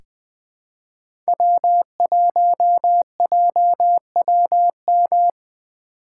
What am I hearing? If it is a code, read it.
Morse code, heard as a keyed tone, about 21 wpm, 700 Hz. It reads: W1JWM
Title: USA Call Signs - 5 Characters - 4x Character Spacing - 20wpm